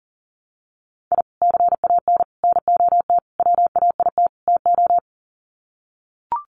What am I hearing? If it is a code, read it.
Morse code, heard as a keyed tone, about 40 wpm, 700 Hz. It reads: I CAN NOT WAIT TO
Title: Sets of 5 Words Encore 40wpm